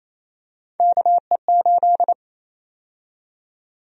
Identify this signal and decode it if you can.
Morse code, heard as a keyed tone, about 28 wpm, 700 Hz. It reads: KE8
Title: USA Call Sign Prefixes - Repeated 2x - 28wpm